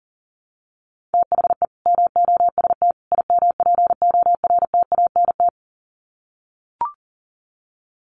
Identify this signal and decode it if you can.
Morse code, heard as a keyed tone, about 40 wpm, 700 Hz. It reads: THE MOST IMPORTANT
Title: Sets of 3 Words 40wpm